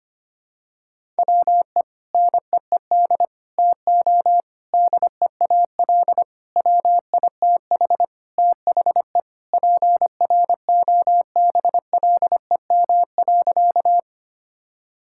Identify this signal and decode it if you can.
Morse code, heard as a keyed tone, about 25 wpm, 700 Hz. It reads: WE NEED TO DEAL WITH THE PROBLEM.
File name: practice / 25wpm / Sentences with 4-Word Phrases - 25wpm